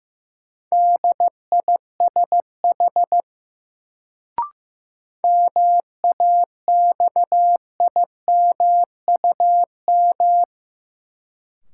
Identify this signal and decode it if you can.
Morse code, heard as a keyed tone, about 15 wpm, 700 Hz. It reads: DISH MAXIMUM
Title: Sentences from Top 1300 Words - New Word Drill Alt - 15wpm